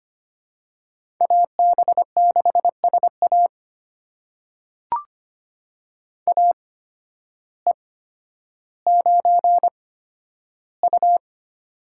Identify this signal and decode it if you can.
Morse code, heard as a keyed tone, about 25 wpm, spaced as for 12 wpm, 700 Hz. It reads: AB6SA AE9U
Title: USA Call Signs - 8x Character Spacing - 25wpm